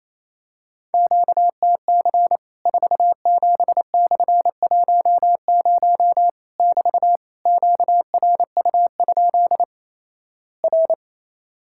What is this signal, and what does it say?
Morse code, heard as a keyed tone, about 28 wpm, 700 Hz. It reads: QTC 47/10 <BT> QRU? R
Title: Worked All Europe DX Contest - QTC Exchange - 28wpm